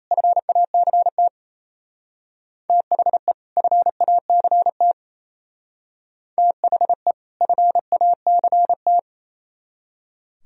Morse code, heard as a keyed tone, about 33 wpm, 700 Hz. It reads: FACT THE FACT THE FACT
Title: Speed-Racing - Sets of 2 Words 28wpm